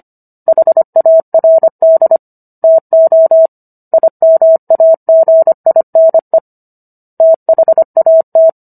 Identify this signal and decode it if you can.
Morse code, heard as a keyed tone, about 25 wpm, 650 Hz. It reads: HARD TO IMAGINE THAT